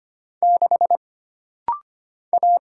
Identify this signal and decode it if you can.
Morse code, heard as a keyed tone, about 25 wpm, 700 Hz. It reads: 6 A
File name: Single Character (Letter:Number:Punctuation:Prosign) - Rapid-Fire - 25wpm